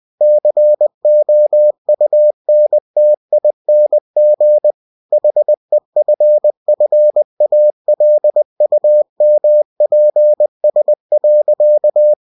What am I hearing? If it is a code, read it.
Morse code, heard as a keyed tone, about 20 wpm, 600 Hz. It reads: COUNTING HEFFALUMPS.
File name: cw-books / winnie-the-pooh / winnie-20wpm-page-0039